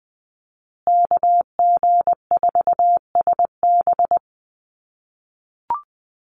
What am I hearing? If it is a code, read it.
Morse code, heard as a keyed tone, about 20 wpm, 700 Hz. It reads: KG4SB
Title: USA Call Signs Repeated 3x 20wpm